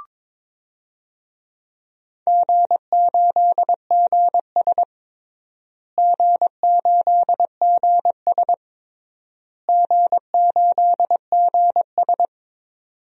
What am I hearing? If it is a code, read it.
Morse code, heard as a keyed tone, about 22 wpm, 700 Hz. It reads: G8GS G8GS G8GS
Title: DXCC-100 Call Signs - 4 Characters - Repeated 3x - 22wpm